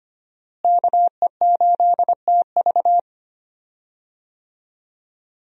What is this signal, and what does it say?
Morse code, heard as a keyed tone, about 25 wpm, 700 Hz. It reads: KE8TV
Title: USA Call Signs - 5 Characters - Repeated 2x - 25wpm